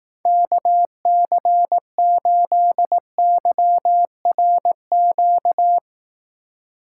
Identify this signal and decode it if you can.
Morse code, heard as a keyed tone, about 18 wpm, 700 Hz. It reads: KC8YRQ
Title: Speed Racing - USA 2x3 Call Signs 15wpm